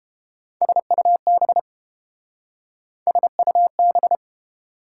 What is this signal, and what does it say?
Morse code, heard as a keyed tone, about 29 wpm, 700 Hz. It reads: SUB SUB